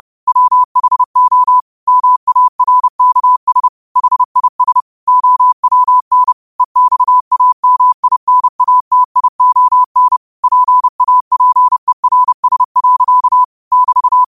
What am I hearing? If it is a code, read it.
Morse code, heard as a keyed tone, about 30 wpm, 1000 Hz. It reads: WHO MARKS HIS OWN EXAMINATION PAPERS. <BT>
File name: qotdfiles / 30wpm